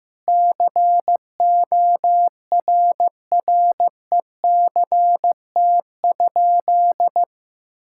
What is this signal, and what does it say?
Morse code, heard as a keyed tone, about 15 wpm, 700 Hz. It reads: CORRECT?
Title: Sentences from Top 1000 Words - Review Part 5 - 15wpm